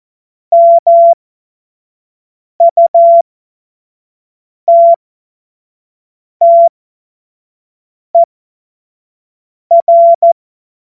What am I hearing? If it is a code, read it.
Morse code, heard as a keyed tone, about 14 wpm, spaced as for 5 wpm, 680 Hz. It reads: MUTTER
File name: Morsekurs / Witze / Witz 06. Dezember 2022 25 F70